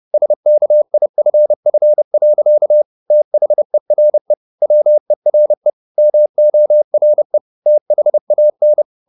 Morse code, heard as a keyed tone, about 30 wpm, 600 Hz. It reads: SKIFF. THERE WERE MORE THAN